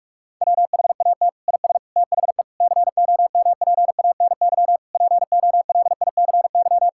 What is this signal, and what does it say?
Morse code, heard as a keyed tone, about 46 wpm, 700 Hz. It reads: WHAT IS THE COMPANY POLICY
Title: Sentences from Top 1000 Words - Review Part 5 - 45wpm